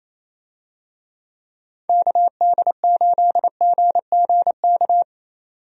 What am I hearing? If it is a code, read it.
Morse code, heard as a keyed tone, about 28 wpm, 700 Hz. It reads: KD8GGK
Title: USA Call Signs - 6 Characters - 28wpm